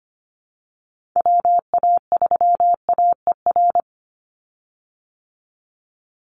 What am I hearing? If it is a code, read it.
Morse code, heard as a keyed tone, about 25 wpm, 700 Hz. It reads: WA3AER
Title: Call Signs Repeated 2x - 25wpm